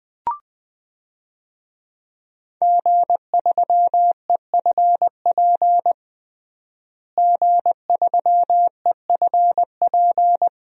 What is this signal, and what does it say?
Morse code, heard as a keyed tone, about 20 wpm, 700 Hz. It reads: G3EFP G3EFP
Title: DXCC-100 Call Signs - 5 Characters - Repeated 3x - 20wpm